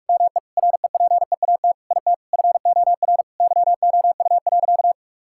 Morse code, heard as a keyed tone, about 43 wpm, 700 Hz. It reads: ME REPEAT IT FOR YOU.
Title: Sentences from Top 1600 Words 45wpm